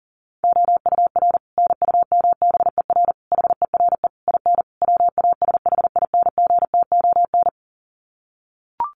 Morse code, heard as a keyed tone, about 40 wpm, 700 Hz. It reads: OUR NUMBER HERE IN WASHINGTON
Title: Sets of 5 Words Encore2 40wpm